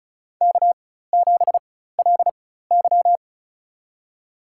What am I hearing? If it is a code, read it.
Morse code, heard as a keyed tone, about 36 wpm, spaced as for 16 wpm, 700 Hz. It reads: K7LY
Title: USA Call Signs - 4 Characters - 4x Character Spacing - 35wpm